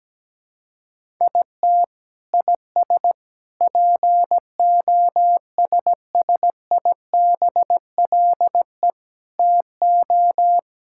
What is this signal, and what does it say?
Morse code, heard as a keyed tone, about 17 wpm, 700 Hz. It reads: IT IS POSSIBLE TO
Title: Sentences from Top 1000 Words - Review Part 6 - 17wpm